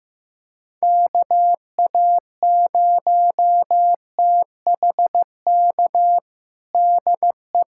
Morse code, heard as a keyed tone, about 15 wpm, 700 Hz. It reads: KA0THK DE
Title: USA Callsign DE Callsign 15wpm